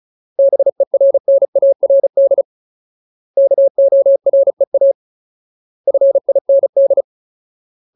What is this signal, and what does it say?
Morse code, heard as a keyed tone, about 34 wpm, 550 Hz. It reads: BERNARD KOREA FIND
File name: Ultimate Word List / 35WPM Ultimate